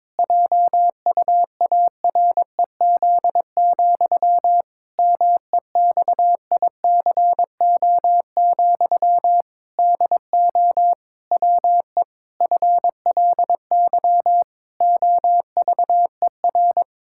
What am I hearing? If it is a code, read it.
Morse code, heard as a keyed tone, about 22 wpm, 700 Hz. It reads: JUAREZ, MEXICO, DO WE FLY OVER
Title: Pangram Challenge v2 - 22wpm